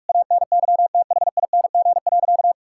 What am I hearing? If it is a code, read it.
Morse code, heard as a keyed tone, about 46 wpm, 700 Hz. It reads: ANYTHING.
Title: Sentences from Top 1000 Words - Review Part 6 - 45wpm